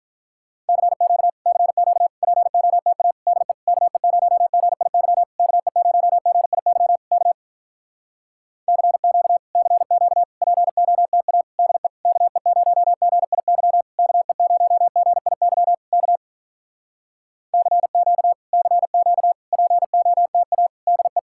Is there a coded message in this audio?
Yes: Morse code from 0.7 to 21.2 s, heard as a keyed tone, about 48 wpm, 700 Hz. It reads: CQ CQ POTA DE KE0GIY KE0GIY K CQ CQ POTA DE KE0GIY KE0GIY K CQ CQ POTA DE